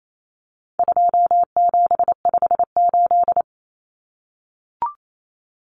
Morse code, heard as a keyed tone, about 28 wpm, 700 Hz. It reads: 2758
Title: Numbers - 4-Digit 28wpm